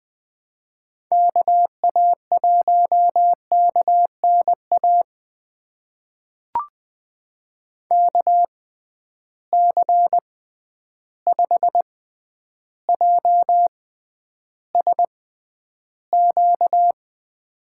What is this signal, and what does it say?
Morse code, heard as a keyed tone, about 20 wpm, spaced as for 10 wpm, 700 Hz. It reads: KA1KNA KC5JSQ
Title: USA Call Signs - 6 Characters - 6x Character Spacing - 20wpm